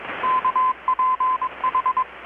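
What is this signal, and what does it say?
Morse code, heard as a keyed tone, about 21 wpm, 1020 Hz, with radio noise. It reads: KPH
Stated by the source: Morse from KPH's Night of Nights XX maritime radiotelegraphy event, received in Chiapas, Mexico about 3,800 km away on a Kenwood R-600 with a grounded 10 m long-wire antenna.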